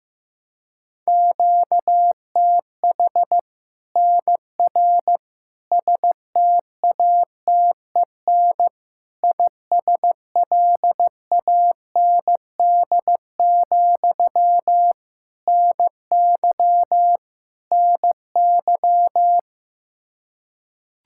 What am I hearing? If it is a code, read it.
Morse code, heard as a keyed tone, about 15 wpm, 700 Hz. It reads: QTH NR STATEN ISLAND, NY NY
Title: Practice QSO - Exchange 1 - Beginning - 15wpm